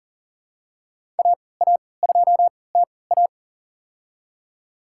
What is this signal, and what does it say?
Morse code, heard as a keyed tone, about 40 wpm, spaced as for 23 wpm, 700 Hz. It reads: AA2TA
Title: USA Call Signs - 3x Character Spacing - 40wpm